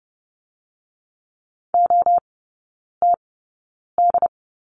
Morse code, heard as a keyed tone, about 30 wpm, spaced as for 9 wpm, 700 Hz. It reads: OTD
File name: course / 30:10wpm / 087 Lesson 15e - Number 5 - Three Characters 30wpm